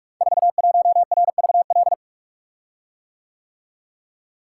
Morse code, heard as a keyed tone, about 44 wpm, 700 Hz. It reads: 41RVL
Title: Code Groups - 5 Letter:Numbers - 45wpm